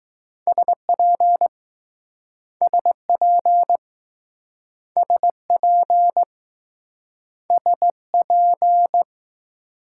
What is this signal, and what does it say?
Morse code, heard as a keyed tone, about 20 wpm, 700 Hz. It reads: SP SP SP SP